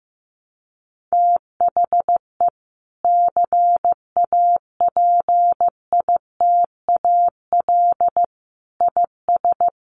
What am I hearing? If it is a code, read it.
Morse code, heard as a keyed tone, about 15 wpm, 700 Hz. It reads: THE CAPITAL IS
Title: Sentences from Top 1000 Words 15wpm